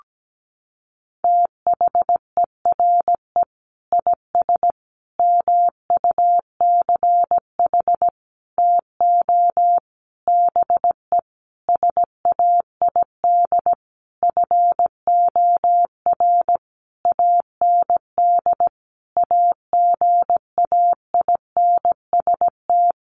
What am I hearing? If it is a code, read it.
Morse code, heard as a keyed tone, about 17 wpm, 700 Hz. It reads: THERE IS MUCH TO BE SAID FOR AND AGAINST
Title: Sentences from Top 1300 Words 17wpm